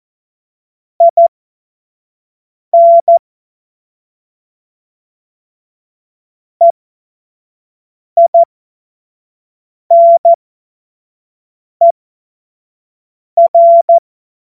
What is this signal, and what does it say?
Morse code, heard as a keyed tone, about 14 wpm, spaced as for 5 wpm, 680 Hz. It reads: IN EINER
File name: Morsekurs / Witze / Witz 11. Januar 2022, 25 F70